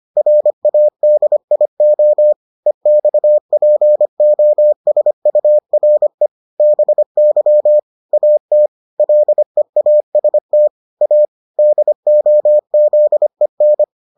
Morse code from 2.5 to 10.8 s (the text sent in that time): EXPOSURE BY AT LEAST